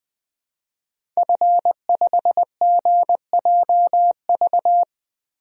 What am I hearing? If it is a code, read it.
Morse code, heard as a keyed tone, about 21 wpm, 700 Hz. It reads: F5GJV
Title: DXCC-100 Call Signs - 5 Characters - Repeated 3x - 20wpm